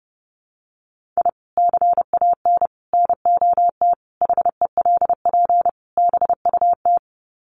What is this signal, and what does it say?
Morse code, heard as a keyed tone, about 30 wpm, 700 Hz. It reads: I CAN NOT HELP BUT
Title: Sets of 5 Words Encore2 30wpm